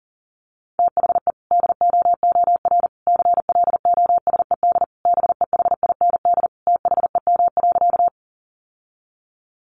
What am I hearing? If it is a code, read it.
Morse code, heard as a keyed tone, about 40 wpm, 700 Hz. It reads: THE DOOR CLOSED BEHIND THEM.
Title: Sentences from Top 1400 Words 40wpm